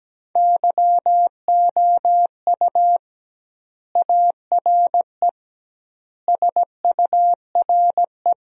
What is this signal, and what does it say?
Morse code, heard as a keyed tone, about 17 wpm, 700 Hz. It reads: YOU ARE SURE
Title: 2x Word-Spacing - Sentences from Top 300 Words - 17wpm